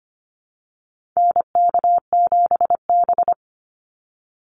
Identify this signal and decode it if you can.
Morse code, heard as a keyed tone, about 25 wpm, 700 Hz. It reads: NK7B
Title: Call Signs Repeated 2x - 25wpm